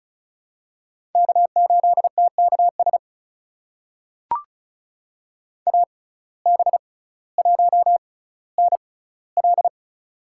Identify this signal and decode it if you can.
Morse code, heard as a keyed tone, about 36 wpm, spaced as for 18 wpm, 700 Hz. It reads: K8TKS AB1NL
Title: USA Call Signs - 5 Characters - 6x Character Spacing - 35wpm